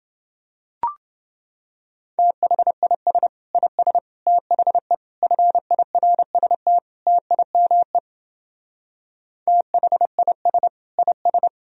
Morse code, heard as a keyed tone, about 28 wpm, 700 Hz. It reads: THIS IS THE FIRST TIME THIS IS